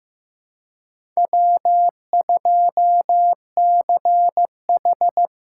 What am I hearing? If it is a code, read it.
Morse code, heard as a keyed tone, about 18 wpm, 700 Hz. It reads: W2CH
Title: Speed Racing - USA 1x2 Call Signs 15wpm